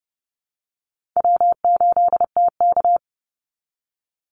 Morse code, heard as a keyed tone, about 30 wpm, 700 Hz. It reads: W8TK
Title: Call Signs Repeated 3x - 30wpm